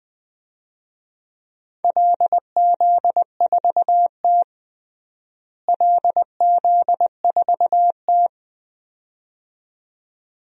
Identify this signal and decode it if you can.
Morse code, heard as a keyed tone, about 21 wpm, 700 Hz. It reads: LZ4T LZ4T
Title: DXCC-100 Call Signs - 4 Characters - Repeated 2x - 20wpm